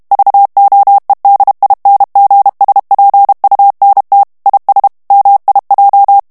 Morse code, heard as a keyed tone, about 32 wpm, 800 Hz. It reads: VOEDINGSPUNT IS MIJ
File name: morse mp3 / HOLLANDAIS texte clair 1 vitesse 30 mots minute